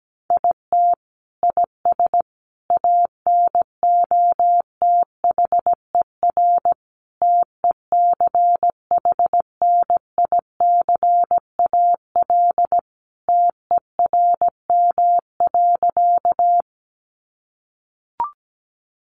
Morse code, heard as a keyed tone, about 17 wpm, 700 Hz. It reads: IT IS ANOTHER TECHNICAL TERM.
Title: Sentences from Top 1100 Words 17wpm